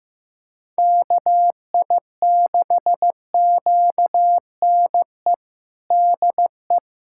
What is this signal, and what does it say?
Morse code, heard as a keyed tone, about 15 wpm, 700 Hz. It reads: KI6QNE DE
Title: USA Callsign DE Callsign 15wpm